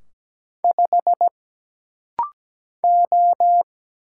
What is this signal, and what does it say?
Morse code, heard as a keyed tone, about 17 wpm, 700 Hz. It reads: 5 O
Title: Single Letter-Number - Rapid-Fire 17wpm